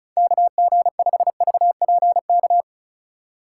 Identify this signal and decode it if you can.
Morse code, heard as a keyed tone, about 36 wpm, 690 Hz. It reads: KG5VPK
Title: USA Call Signs - 6 Characters - 6x Character Spacing - 35wpm